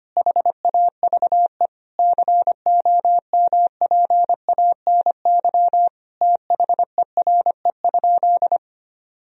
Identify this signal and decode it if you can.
Morse code, heard as a keyed tone, about 25 wpm, 700 Hz. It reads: HAVE COMPANY THERE?
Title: Sentences from Top 1000 Words - Review Part 3 - 25wpm